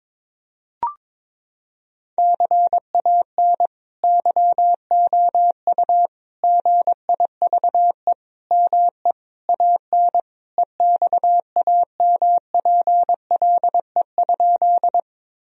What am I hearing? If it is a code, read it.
Morse code, heard as a keyed tone, about 21 wpm, 700 Hz. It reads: CAN YOU GIVE ME AN EXAMPLE?
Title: Sentences from Top 1000 Words - Review Part 1 - 22wpm